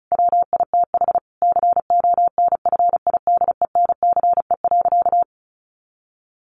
Morse code, heard as a keyed tone, about 34 wpm, 700 Hz. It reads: WITH CONFIDENCE.
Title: Sentences from Top 1500 Words 35wpm